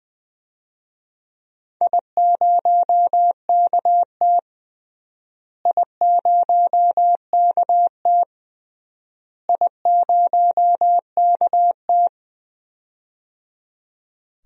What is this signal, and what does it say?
Morse code, heard as a keyed tone, about 20 wpm, 700 Hz. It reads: I0KT I0KT I0KT
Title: DXCC-100 Call Signs - 4 Characters - Repeated 3x - 20wpm